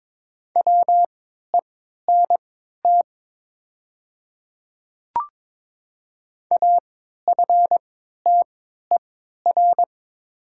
Morse code, heard as a keyed tone, about 23 wpm, spaced as for 13 wpm, 700 Hz. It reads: WENT AFTER
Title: 3x Character Spacing - Top 300 Words - 22wpm